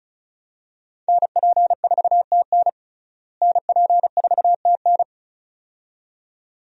Morse code, heard as a keyed tone, about 34 wpm, 700 Hz. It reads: NP4TN NP4TN
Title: USA Call Signs - 5 Characters - Repeated 2x - 35wpm